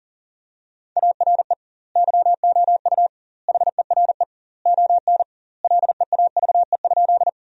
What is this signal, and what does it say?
Morse code, heard as a keyed tone, about 40 wpm, 700 Hz. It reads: ARE YOU HERE ON LEAVE?
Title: 2x Word-Spacing - Sentences from Top 400 Words - 40wpm